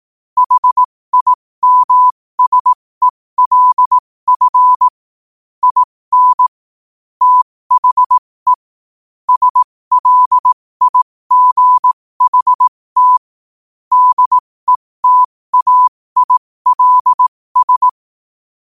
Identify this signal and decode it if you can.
Morse code, heard as a keyed tone, about 18 wpm, spaced as for 15 wpm, 1000 Hz. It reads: HIMSELF IN THE SLIGHT DETAILS
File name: qotdfiles / 15wpm